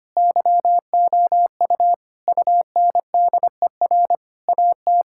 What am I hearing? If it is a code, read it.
Morse code, heard as a keyed tone, about 25 wpm, 700 Hz. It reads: YOU UNDER AT
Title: Sentences from Top 1000 Words - Review Part 6 - 25wpm